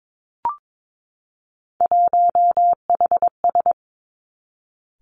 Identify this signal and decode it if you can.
Morse code, heard as a keyed tone, about 22 wpm, 700 Hz. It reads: 1HS
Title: Three Challenging Letter-Number Combinations 22wpm